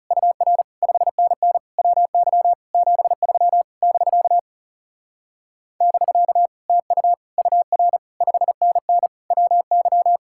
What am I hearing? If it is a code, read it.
Morse code, heard as a keyed tone, about 40 wpm, 700 Hz. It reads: UR 5NN WY 73 <BK> <BK> TU UR 5NN WY